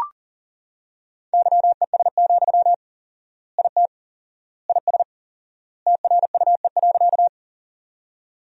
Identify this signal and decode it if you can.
Morse code, heard as a keyed tone, about 37 wpm, 700 Hz. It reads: YES, IT IS TRUE.
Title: 4x Word-Spacing - Sentences from Top 500 Words - 40wpm